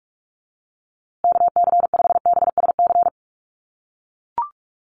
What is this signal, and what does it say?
Morse code, heard as a keyed tone, about 44 wpm, 700 Hz. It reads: KC5BSC
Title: USA Call Signs Repeated 3x 45wpm